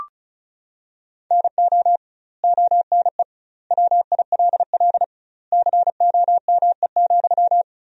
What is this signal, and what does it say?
Morse code, heard as a keyed tone, about 36 wpm, 700 Hz. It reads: NO ONE WILL COME,
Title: 2x Word-Spacing - Sentences from Top 100 Words - 35wpm